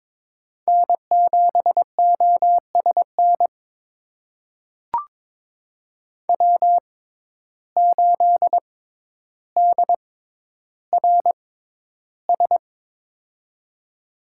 Morse code, heard as a keyed tone, about 21 wpm, spaced as for 11 wpm, 700 Hz. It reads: N7OSN W8DRS
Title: USA Call Signs - 5 Characters - 6x Character Spacing - 22wpm